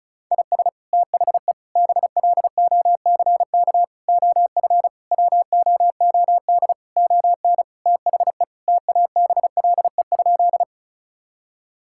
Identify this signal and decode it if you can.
Morse code, heard as a keyed tone, about 36 wpm, 700 Hz. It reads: IS THE BLOCK OF WOOD ON THE TABLE?
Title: Sentences from Top 1600 Words 35wpm